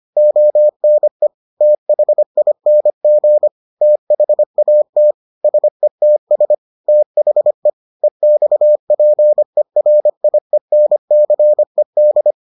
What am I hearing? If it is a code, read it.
Morse code, heard as a keyed tone, about 25 wpm, 600 Hz. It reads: ONE THING THAT SETS THE EXPERIENCED